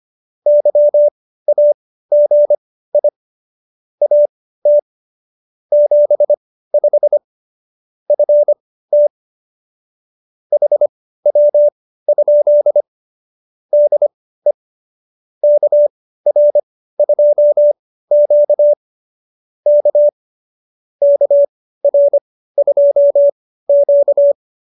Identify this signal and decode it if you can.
Morse code, heard as a keyed tone, about 25 wpm, spaced as for 15 wpm, 590 Hz. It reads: YAGI AT 75 FT HW? DE KR2Q K KR2Q